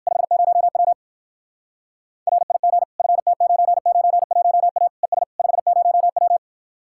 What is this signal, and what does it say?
Morse code, heard as a keyed tone, about 56 wpm, 700 Hz. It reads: 50W RIG FT991A ES 50W